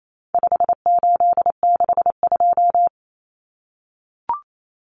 Morse code, heard as a keyed tone, about 28 wpm, 700 Hz. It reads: 5862
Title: Common Cut Numbers - 4-Digit 28wpm